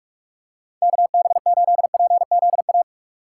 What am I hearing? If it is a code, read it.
Morse code, heard as a keyed tone, about 46 wpm, 700 Hz. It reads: KB8PZA
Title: USA Call Signs - 6 Characters - 4x Character Spacing - 45wpm